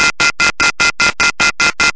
Off-hook warning tone, 1400 + 2060 + 2450 + 2600 Hz, with noise.
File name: SOUND EFFECTS and NOISES / Phone Tones / 90[kb]off-da-hook.wav